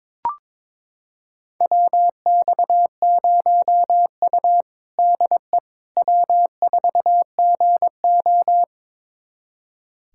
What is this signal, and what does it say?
Morse code, heard as a keyed tone, about 21 wpm, 700 Hz. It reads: WX0U DE W4GO
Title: USA Callsign DE Callsign - 4 Characters - 22wpm